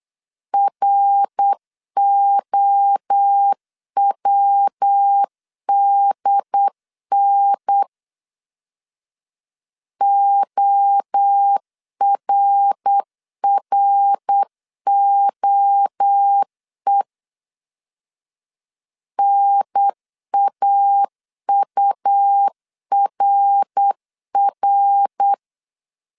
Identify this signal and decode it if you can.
Morse code, heard as a keyed tone, about 8 wpm, 800 Hz. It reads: ROWDN ORROE NAURR
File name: CW / DET / 1 3 1500C S B 8wpm 4141